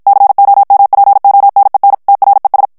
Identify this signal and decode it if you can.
Morse code, heard as a keyed tone, about 54 wpm, 800 Hz. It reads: COMPONEN TRES
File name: ESPAGNOL TEXTE EN CLAIR 1 VITESSE 50 MOTS MINUTES